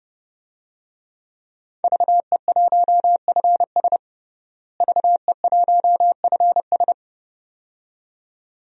Morse code, heard as a keyed tone, about 30 wpm, 700 Hz. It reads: VE1FS VE1FS
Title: DXCC-100 Call Signs - 5 Characters - Repeated 2x - 30wpm